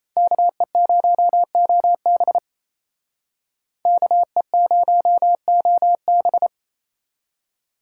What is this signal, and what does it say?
Morse code, heard as a keyed tone, about 33 wpm, 700 Hz. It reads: KE0OB KE0OB
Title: Speed Racing - USA 2x2 Call Signs 28wpm